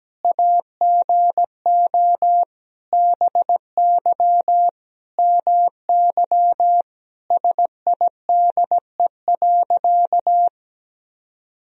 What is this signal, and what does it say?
Morse code, heard as a keyed tone, about 17 wpm, 700 Hz. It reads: AGO BY MY SIDE.
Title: Sentences from Top 1000 Words - Review Part 3 - 17wpm